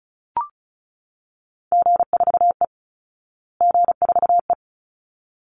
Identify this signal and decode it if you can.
Morse code, heard as a keyed tone, about 34 wpm, 700 Hz. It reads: G4E G4E
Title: Call Signs Repeated 3x - 35wpm